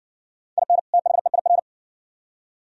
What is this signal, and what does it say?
Morse code, heard as a keyed tone, about 60 wpm, 700 Hz. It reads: IN THEIR